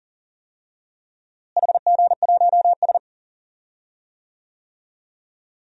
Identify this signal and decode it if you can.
Morse code, heard as a keyed tone, about 38 wpm, 700 Hz. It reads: HG1S
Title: DXCC-100 Call Signs - 4 Characters - Repeated 2x - 40wpm